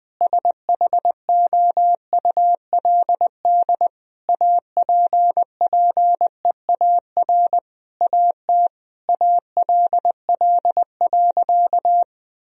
Morse code, heard as a keyed tone, about 20 wpm, 700 Hz. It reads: SHOULD APPEAR AT ALL.
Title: Sentences from Top 1000 Words - Review Part 2 - 20wpm